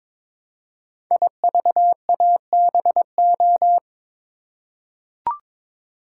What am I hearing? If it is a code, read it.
Morse code, heard as a keyed tone, about 23 wpm, 700 Hz. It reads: IVABO
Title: Code Groups - 5 Letters - 22wpm